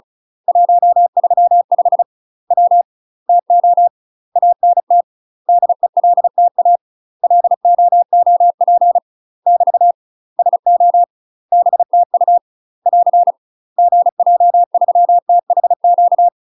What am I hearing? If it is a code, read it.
Morse code, heard as a keyed tone, about 36 wpm, 700 Hz. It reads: 135 W TO ANT DELTA LOOP <BT> SO BTU <AR> GJ3THQ
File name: UK MASTER QSOs TRAINING / 35WPM / UK MASTER QSOs 0016